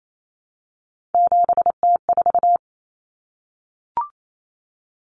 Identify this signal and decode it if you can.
Morse code, heard as a keyed tone, about 28 wpm, 700 Hz. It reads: 7T4
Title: Common Cut Numbers - 3-Digit 28wpm